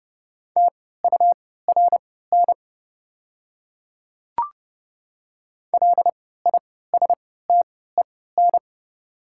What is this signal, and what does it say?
Morse code, heard as a keyed tone, about 30 wpm, spaced as for 17 wpm, 700 Hz. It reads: TURN LISTEN
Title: 3x Character Spacing - Top 500 Words - 30wpm